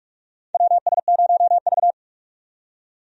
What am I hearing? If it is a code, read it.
Morse code, heard as a keyed tone, about 46 wpm, 700 Hz. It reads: WS0V
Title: USA Call Signs - 4 Characters - 45wpm